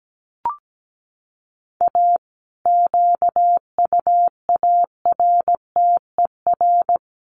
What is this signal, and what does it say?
Morse code, heard as a keyed tone, about 17 wpm, 700 Hz. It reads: A QUARTER
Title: Sets of 5 Words Encore2 17wpm